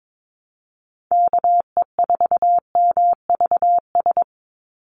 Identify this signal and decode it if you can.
Morse code, heard as a keyed tone, about 22 wpm, 700 Hz. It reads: KE4MVS
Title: USA Call Signs 22wpm